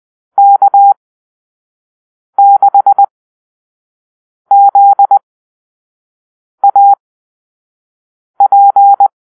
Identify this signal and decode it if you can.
Morse code, heard as a keyed tone, about 20 wpm, spaced as for 5 wpm, 800 Hz. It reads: K6ZAP
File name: arrl / Call-Sign Drill-1